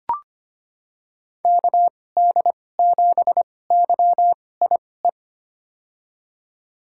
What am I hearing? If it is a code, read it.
Morse code, heard as a keyed tone, about 25 wpm, spaced as for 18 wpm, 700 Hz. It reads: KD7YIE
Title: USA Call Signs - 6 Characters - 2x Character Spacing - 25wpm